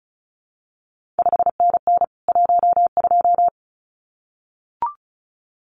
Morse code, heard as a keyed tone, about 36 wpm, 700 Hz. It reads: 5NN 12
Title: CQ WW DX Contest Exchange Encore - 35wpm